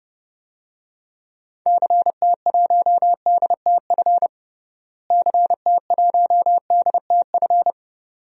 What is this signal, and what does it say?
Morse code, heard as a keyed tone, about 32 wpm, 700 Hz. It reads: CT1DTF CT1DTF
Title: DXCC-100 Call Signs - 6 Characters - Repeated 2x - 30wpm